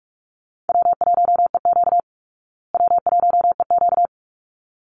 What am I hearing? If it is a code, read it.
Morse code, heard as a keyed tone, about 44 wpm, 700 Hz. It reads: W1EQ W1EQ
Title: Call Signs Repeated 3x - 45wpm